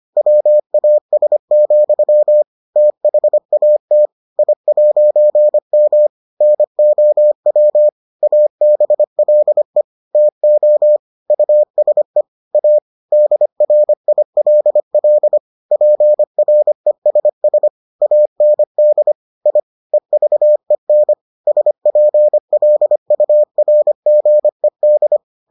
Morse code, heard as a keyed tone, about 25 wpm, 600 Hz. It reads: WAS, THAT I'M NOW ABLE TO USE A DRILL PRESS AND I EVEN SPLURGED